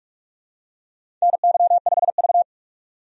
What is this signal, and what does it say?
Morse code, heard as a keyed tone, about 47 wpm, 700 Hz. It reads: NY5V
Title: USA Call Signs - 4 Characters - 45wpm